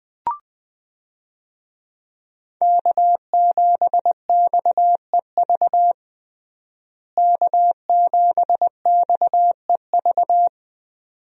Morse code, heard as a keyed tone, about 20 wpm, 700 Hz. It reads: K7XEV K7XEV
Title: USA Call Signs - 5 Characters - Repeated 2x - 20wpm